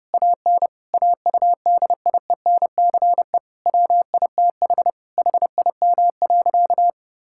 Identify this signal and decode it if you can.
Morse code, heard as a keyed tone, about 30 wpm, 700 Hz. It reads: AN AUDIENCE WITH HIM.
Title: Sentences from Top 1000 Words - Review Part 5 - 30wpm